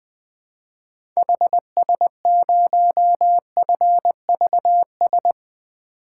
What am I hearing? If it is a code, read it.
Morse code, heard as a keyed tone, about 20 wpm, 700 Hz. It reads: HS0FVS
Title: 2023 POTA Activator Call Signs - 20wpm